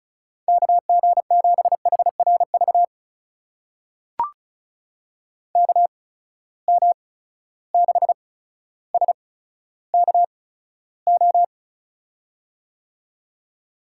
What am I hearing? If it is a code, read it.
Morse code, heard as a keyed tone, about 36 wpm, spaced as for 15 wpm, 700 Hz. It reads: KG7HRV KM6SKO
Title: USA Call Signs - 6 Characters - 8x Character Spacing - 35wpm